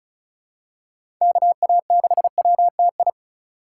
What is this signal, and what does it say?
Morse code, heard as a keyed tone, about 36 wpm, 700 Hz. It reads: KA6WTI
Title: USA Call Signs - 6x Character Spacing - 35wpm